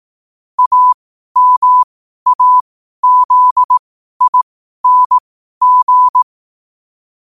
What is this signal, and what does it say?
Morse code, heard as a keyed tone, about 18 wpm, spaced as for 12 wpm, 1000 Hz. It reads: AMAZING
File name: qotdfiles / 12wpm